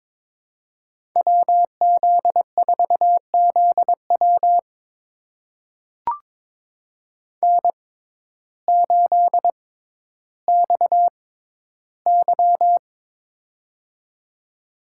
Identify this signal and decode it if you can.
Morse code, heard as a keyed tone, about 22 wpm, spaced as for 12 wpm, 700 Hz. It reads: WZ4ZW N8XY
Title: USA Call Signs - 6x Character Spacing - 22wpm